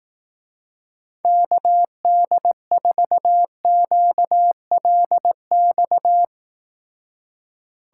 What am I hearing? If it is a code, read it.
Morse code, heard as a keyed tone, about 18 wpm, 700 Hz. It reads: KD4QLX
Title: Speed Racing - USA 2x3 Call Signs 15wpm